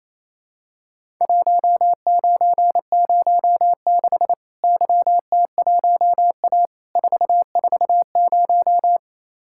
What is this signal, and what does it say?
Morse code, heard as a keyed tone, about 28 wpm, 700 Hz. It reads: 1906 YT1A 440
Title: Worked All Europe DX Contest - QTC Exchange - 28wpm